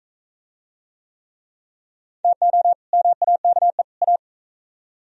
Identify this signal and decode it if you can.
Morse code, heard as a keyed tone, about 39 wpm, 700 Hz. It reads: TO MAKE A